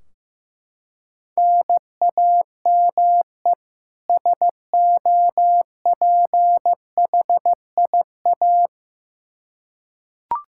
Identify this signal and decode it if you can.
Morse code, heard as a keyed tone, about 15 wpm, 700 Hz. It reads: NAME SOPHIA
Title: Practice QSO - Exchange 1 - Beginning - 15wpm